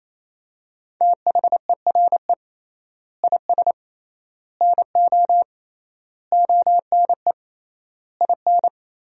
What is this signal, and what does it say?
Morse code, heard as a keyed tone, about 28 wpm, 700 Hz. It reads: THERE IS NO ONE IN
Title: 3x Word-Spacing - Sentences from Top 500 Words - 28wpm